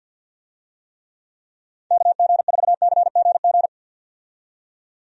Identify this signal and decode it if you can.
Morse code, heard as a keyed tone, about 52 wpm, 700 Hz. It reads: KG4CGG
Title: USA Call Signs - 6 Characters - 50wpm